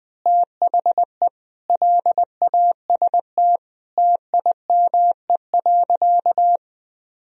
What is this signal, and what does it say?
Morse code, heard as a keyed tone, about 20 wpm, 700 Hz. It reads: THE LAST TIME.
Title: Sentences from Top 1000 Words - Review Part 2 - 20wpm